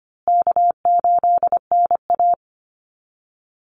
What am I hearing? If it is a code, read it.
Morse code, heard as a keyed tone, about 25 wpm, 700 Hz. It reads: K8NA
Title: Easy Call Signs Repeated 3x - 25wpm